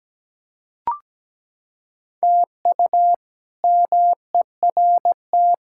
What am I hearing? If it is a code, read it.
Morse code, heard as a keyed tone, about 17 wpm, 700 Hz. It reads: TU MERT
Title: K1USN SST QSO Practice - 17wpm